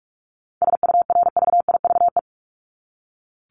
Morse code, heard as a keyed tone, about 44 wpm, 700 Hz. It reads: SURVIVE
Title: Top 1251-1500 Words 45wpm